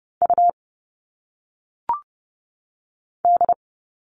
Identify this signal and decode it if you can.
Morse code, heard as a keyed tone, about 30 wpm, 700 Hz. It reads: U D
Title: All Cut Numbers - 1-Digit 30wpm